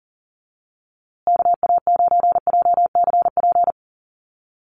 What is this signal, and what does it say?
Morse code, heard as a keyed tone, about 40 wpm, 700 Hz. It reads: KA9JCP
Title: USA Call Signs Repeated 2x 40wpm